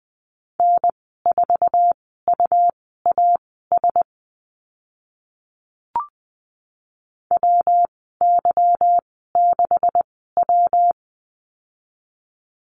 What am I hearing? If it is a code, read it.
Morse code, heard as a keyed tone, about 20 wpm, spaced as for 15 wpm, 700 Hz. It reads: N4UAS WY6W
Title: 147 Lesson 24g - Letter y - Call Signs 20wpm